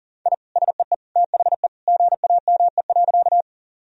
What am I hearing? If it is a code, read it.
Morse code, heard as a keyed tone, about 37 wpm, 700 Hz. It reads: I SEE THE GAME.
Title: Sentences from Top 1000 Words - Review Part 3 - 40wpm